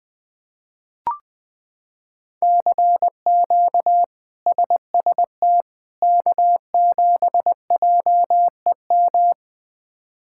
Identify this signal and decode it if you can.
Morse code, heard as a keyed tone, about 21 wpm, 700 Hz. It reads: CQ SST K7JEM
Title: K1USN SST QSO Practice - 20wpm